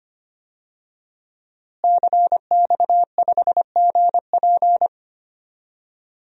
Morse code, heard as a keyed tone, about 25 wpm, 700 Hz. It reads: CX5GP
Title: DXCC-100 Call Signs - 5 Characters - Repeated 2x - 25wpm